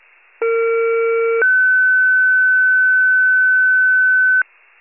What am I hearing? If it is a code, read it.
Two-tone page, heard as a one-second tone, then a three-second tone, about 460 then 1500 Hz. It is heard from 0.4 to 4.4 s.